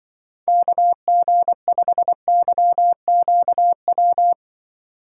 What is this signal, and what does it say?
Morse code, heard as a keyed tone, about 25 wpm, 700 Hz. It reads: KG5YQW
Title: Speed Racing - USA 2x3 Call Signs 20wpm